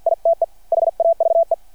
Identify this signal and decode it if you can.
Morse code, heard as a keyed tone, about 51 wpm, 660 Hz. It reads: ITE HAVE
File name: archived / CRSnet